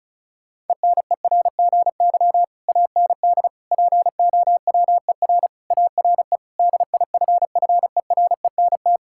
Morse code, heard as a keyed tone, about 34 wpm, 700 Hz. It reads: ENERGY AND POWER ARE DIFFERENT
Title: Sentences from Top 1000 Words - Review Part 6 - 35wpm